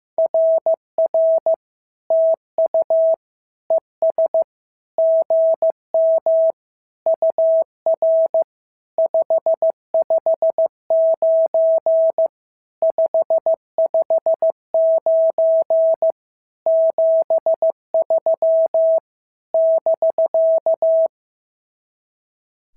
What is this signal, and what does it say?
Morse code, heard as a keyed tone, about 14 wpm, 650 Hz. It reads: RR TU ES GM UR 559 559 73 <BK>
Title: SOTA QSO Practice 15wpm